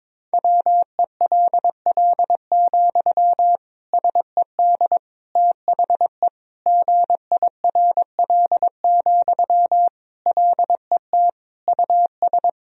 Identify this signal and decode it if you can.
Morse code, heard as a keyed tone, about 22 wpm, 700 Hz. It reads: WELL, SED THE GIRL, LET US
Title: Sentences from Top 400 Words and Abbreviations 22wpm